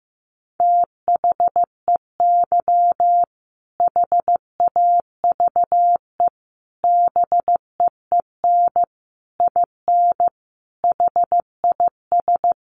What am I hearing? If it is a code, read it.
Morse code, heard as a keyed tone, about 15 wpm, 700 Hz. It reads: THEY HAVE BEEN IN HIS
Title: Sentences from Top 700 Words 15wpm